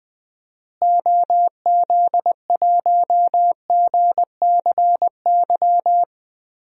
Morse code, heard as a keyed tone, about 20 wpm, 700 Hz. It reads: OZ1GCY
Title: DXCC-100 Call Signs - 6 Characters - Repeated 3x - 20wpm